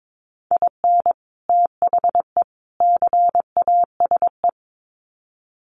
Morse code, heard as a keyed tone, about 22 wpm, 700 Hz. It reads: IN THE CASE
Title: Sets of 3 Words 22wpm